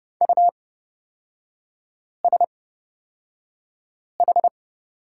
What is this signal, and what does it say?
Morse code, heard as a keyed tone, about 31 wpm, spaced as for 5 wpm, 690 Hz. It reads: USH
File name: Lesson 37e - Letter x - Three Characters 30wpm